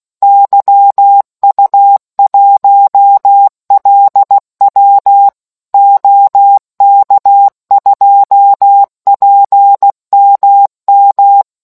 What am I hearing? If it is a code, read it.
Morse code, heard as a keyed tone, about 16 wpm, 800 Hz. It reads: YU1LW OK2PMM